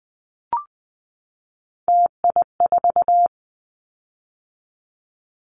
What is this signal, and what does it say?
Morse code, heard as a keyed tone, about 20 wpm, 700 Hz. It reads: TI4
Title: Callsign Prefixes for Top 100 DXCC - 20wpm